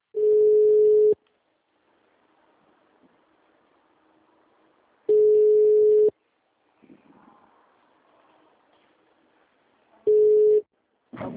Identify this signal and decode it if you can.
Ringback tone (Europe, 425 Hz).